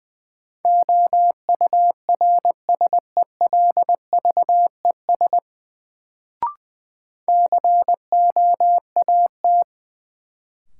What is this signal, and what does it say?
Morse code, heard as a keyed tone, about 20 wpm, 700 Hz. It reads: OURSELVES COAT
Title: Sentences from Top 1600 Words - New Word Drill - 20wpm